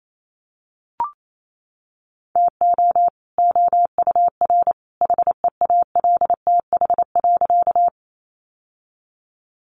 Morse code, heard as a keyed tone, about 28 wpm, 700 Hz. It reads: TO OUR HEALTH.
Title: Sentences from Top 600 Words 28wpm